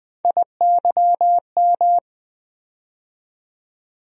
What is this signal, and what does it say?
Morse code, heard as a keyed tone, about 20 wpm, 700 Hz. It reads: IYM
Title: USA Call Sign Postfixes - 3 Letters 20wpm